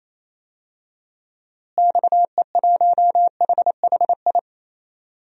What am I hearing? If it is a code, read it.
Morse code, heard as a keyed tone, about 28 wpm, 700 Hz. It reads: XE1HHI
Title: DXCC-100 Call Signs - 6 Characters - Repeated 3x - 28wpm